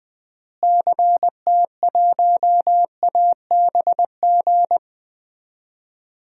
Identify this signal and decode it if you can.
Morse code, heard as a keyed tone, about 21 wpm, 700 Hz. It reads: CT1ABG
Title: DXCC-100 Call Signs - 6 Characters - Repeated 2x - 20wpm